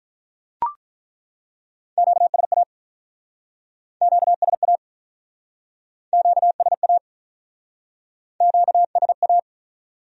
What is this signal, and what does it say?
Morse code, heard as a keyed tone, about 47 wpm, 700 Hz. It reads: QSA QSA QSA QSA